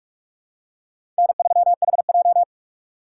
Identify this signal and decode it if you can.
Morse code, heard as a keyed tone, about 44 wpm, 700 Hz. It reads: N3HJ
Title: USA Call Signs - 4 Characters - 2x Character Spacing - 45wpm